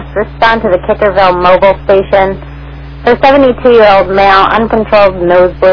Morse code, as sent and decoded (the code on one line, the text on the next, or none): none